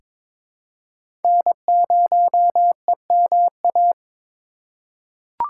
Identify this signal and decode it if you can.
Morse code, heard as a keyed tone, about 22 wpm, 700 Hz. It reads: N0EMA
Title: USA Call Signs - 5 Characters - 4x Character Spacing - 22wpm